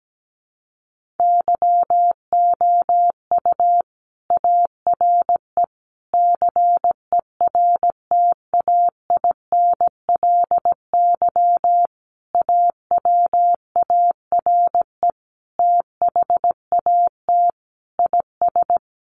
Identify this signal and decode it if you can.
Morse code, heard as a keyed tone, about 17 wpm, 700 Hz. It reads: YOU ARE CERTAINLY AWARE THAT IS
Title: Sentences from Top 1200 Words 17wpm